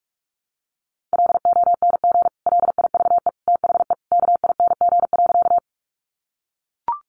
Morse code, heard as a keyed tone, about 44 wpm, 700 Hz. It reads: LONG LIVE THE KING.
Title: Sentences from Top 400 Words 45wpm